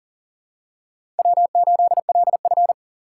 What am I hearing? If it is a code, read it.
Morse code, heard as a keyed tone, about 40 wpm, 700 Hz. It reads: W8LF
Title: USA Call Signs - 4 Characters - 6x Character Spacing - 40wpm